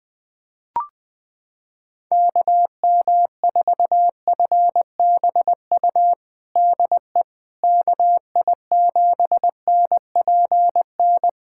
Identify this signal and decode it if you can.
Morse code, heard as a keyed tone, about 20 wpm, 700 Hz. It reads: KM4FBU DE KI7NPN
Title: USA Callsign DE Callsign 20wpm